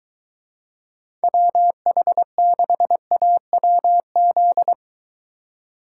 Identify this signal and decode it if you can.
Morse code, heard as a keyed tone, about 22 wpm, 700 Hz. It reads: WH6AWZ